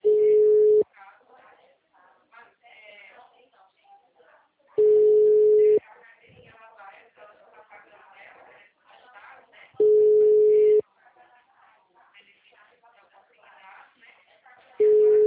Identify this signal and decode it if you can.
Ringback tone (Europe, 425 Hz), with speech.